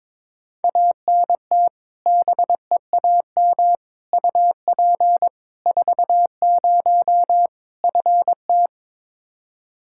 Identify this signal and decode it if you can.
Morse code, heard as a keyed tone, about 22 wpm, 700 Hz. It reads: ANT BEAM UP 40 FT
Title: Practice QSO - Exchange 2 - Info - 22wpm